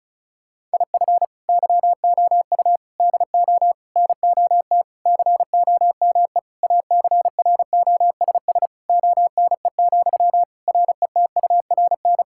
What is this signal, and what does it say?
Morse code, heard as a keyed tone, about 34 wpm, 690 Hz. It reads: IF YOU DO NOT COME ACROSS ONE, RETURN
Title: Sentences from Top 1000 Words - Review Part 5 - 35wpm